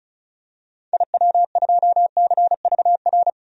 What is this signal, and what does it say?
Morse code, heard as a keyed tone, about 36 wpm, 700 Hz. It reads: IW2CVR
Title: DXCC-100 Call Signs - 6 Characters - 35wpm